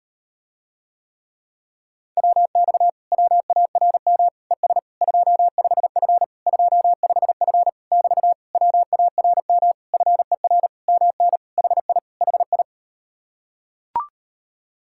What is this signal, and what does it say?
Morse code, heard as a keyed tone, about 38 wpm, 700 Hz. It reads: WX WARM ES 25F 25F <BT> WARM FER MN HI HI